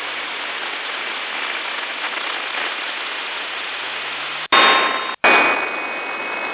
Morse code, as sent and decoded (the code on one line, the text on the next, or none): none